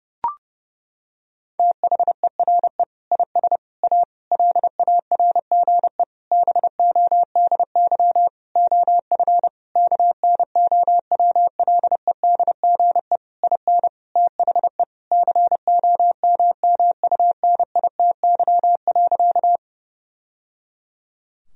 Morse code, heard as a keyed tone, about 30 wpm, 700 Hz. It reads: THERE IS A LARGE BODY OF KNOWLEDGE IN THE COMMUNITY.
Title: Sentences from Top 1000 Words - Review Part 4 - 30wpm